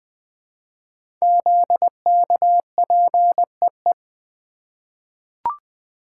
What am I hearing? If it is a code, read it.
Morse code, heard as a keyed tone, about 20 wpm, 700 Hz. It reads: ZKPEE
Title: Code Groups - 5 Letters - 20wpm